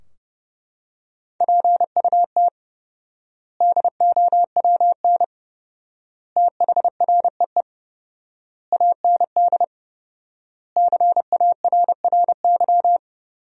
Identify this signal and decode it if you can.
Morse code, heard as a keyed tone, about 30 wpm, 700 Hz. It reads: PUT DOWN THREE AND CARRY
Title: 4x Word-Spacing - Sentences from Top 400 Words - 30wpm